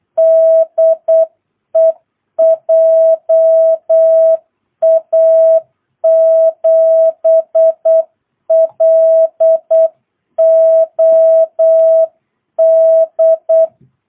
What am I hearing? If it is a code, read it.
Morse code, heard as a keyed tone, about 8 wpm, 660 Hz. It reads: DE JA7LOD
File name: jh8cbh / 2022cw